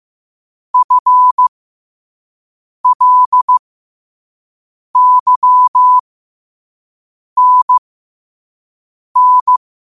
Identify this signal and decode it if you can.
Morse code, heard as a keyed tone, about 15 wpm, spaced as for 5 wpm, 1000 Hz. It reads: FLYNN
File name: qotdfiles / 5wpm